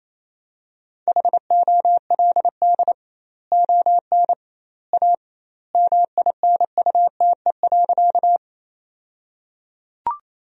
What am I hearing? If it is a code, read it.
Morse code, heard as a keyed tone, about 28 wpm, 700 Hz. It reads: HOLD ON A MINUTE.
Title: 2x Word-Spacing - Sentences from Top 500 Words - 28wpm